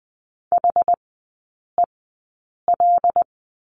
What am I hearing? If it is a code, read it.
Morse code, heard as a keyed tone, about 20 wpm, spaced as for 8 wpm, 700 Hz. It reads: HEL
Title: 121 Lesson 20e - Number 3 - Three Characters 20wpm